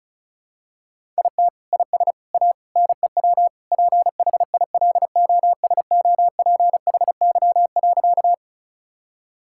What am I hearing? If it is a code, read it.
Morse code, heard as a keyed tone, about 36 wpm, 700 Hz. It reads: IT IS A NEW PHILOSOPHY.
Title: Sentences from Top 1000 Words - Review Part 6 - 35wpm